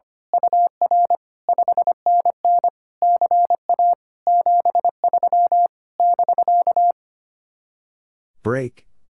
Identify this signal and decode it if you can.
Morse code, heard as a keyed tone, about 24 wpm, 700 Hz. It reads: UR 5NN CA 73 <BK>
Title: Speed-Racing Common QSO Phrases - 25wpm